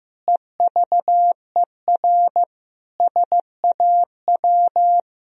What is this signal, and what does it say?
Morse code, heard as a keyed tone, about 15 wpm, 700 Hz. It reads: EVER SAW
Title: Sentences from Top 1000 Words - Review Part 2 - 15wpm